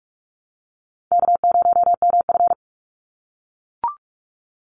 Morse code, heard as a keyed tone, about 44 wpm, 700 Hz. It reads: K0MF
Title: USA Call Signs 45wpm